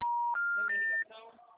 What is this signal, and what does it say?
Special information tone, with speech.